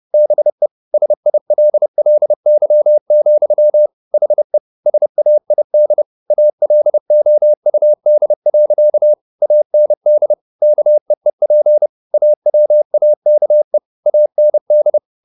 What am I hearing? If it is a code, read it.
Morse code, heard as a keyed tone, about 30 wpm, 600 Hz. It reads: BE SILLY, HE SAID ALOUD. AND KEEP AWAKE AND